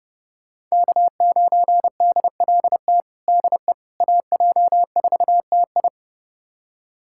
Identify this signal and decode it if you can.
Morse code, heard as a keyed tone, about 30 wpm, 700 Hz. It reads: K9DLT DE AJ4TI
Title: USA Callsign DE Callsign - 5 Characters - 30wpm